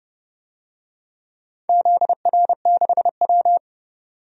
Morse code, heard as a keyed tone, about 29 wpm, 700 Hz. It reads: ZR6W
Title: DXCC-100 Call Signs - 4 Characters - 30wpm